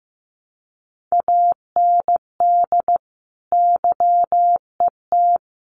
Morse code, heard as a keyed tone, about 15 wpm, 700 Hz. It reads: AND YET
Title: Sentences from Top 500 Words 15wpm